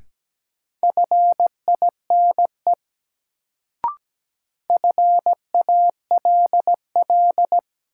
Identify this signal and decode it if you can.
Morse code, heard as a keyed tone, about 17 wpm, 700 Hz. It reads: FINE FALL
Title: Sentences from Top 500 Words - New Word Drill - 17wpm